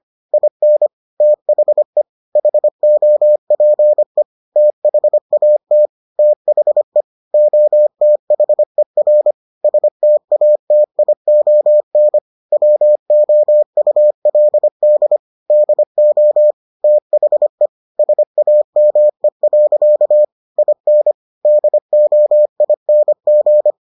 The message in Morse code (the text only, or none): IN THE HOPE THAT THE OTHER STATION WOULD DO THE SAME. IN DOING